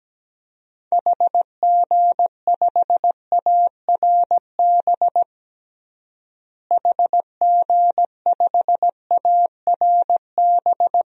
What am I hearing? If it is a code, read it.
Morse code, heard as a keyed tone, about 17 wpm, 700 Hz. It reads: HG5ARB HG5ARB
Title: DXCC-100 Call Signs - 6 Characters - Repeated 2x - 17wpm